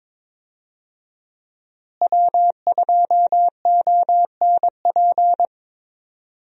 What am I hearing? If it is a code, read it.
Morse code, heard as a keyed tone, about 22 wpm, 700 Hz. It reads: W2ONP
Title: USA Call Signs - 5 Characters - Repeated 3x - 22wpm